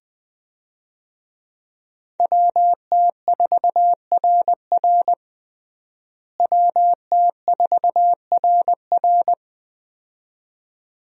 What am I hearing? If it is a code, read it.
Morse code, heard as a keyed tone, about 20 wpm, 700 Hz. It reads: WT4RR WT4RR
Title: USA Call Signs - 5 Characters - Repeated 2x - 20wpm